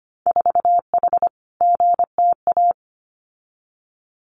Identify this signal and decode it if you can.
Morse code, heard as a keyed tone, about 25 wpm, 700 Hz. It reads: 4H GTA
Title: Winter Field Day Exchanges Encore 25wpm